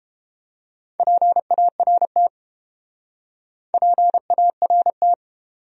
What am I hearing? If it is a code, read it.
Morse code, heard as a keyed tone, about 30 wpm, 700 Hz. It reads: PART PART